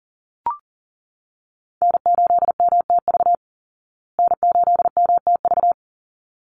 Morse code, heard as a keyed tone, about 40 wpm, 700 Hz. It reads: N8MTV N8MTV
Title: Call Signs Repeated 3x - 40wpm